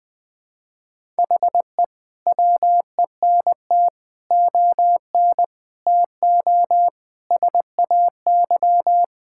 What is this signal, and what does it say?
Morse code, heard as a keyed tone, about 20 wpm, 700 Hz. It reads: HE WENT ON TO SAY
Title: Sets of 6 Words - 20wpm